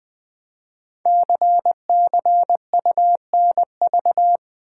Morse code, heard as a keyed tone, about 20 wpm, 700 Hz. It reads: CCUNV
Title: Code Groups - 5 Letters - 20wpm